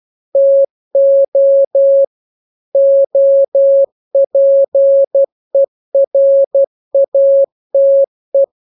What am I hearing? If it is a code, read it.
Morse code, heard as a keyed tone, about 12 wpm, 550 Hz. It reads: TO OPERATE
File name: jokes / joke 5 12 0000